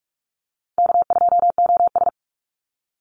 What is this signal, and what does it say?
Morse code, heard as a keyed tone, about 44 wpm, 700 Hz. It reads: K2OS
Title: Call Signs Repeated 2x - 45wpm